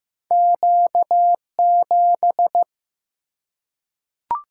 Morse code, heard as a keyed tone, about 15 wpm, 700 Hz. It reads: Q7
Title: Single-Letter Single-Number 15wpm